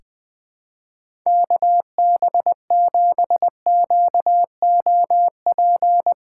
Morse code, heard as a keyed tone, about 20 wpm, 700 Hz. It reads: KB7QOP
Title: K1USN SST QSO Practice - 20wpm